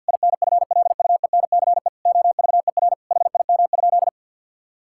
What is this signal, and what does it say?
Morse code, heard as a keyed tone, about 50 wpm, 700 Hz. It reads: INFLUENCE OVER HIM?
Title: Sentences from Top 1600 Words 50wpm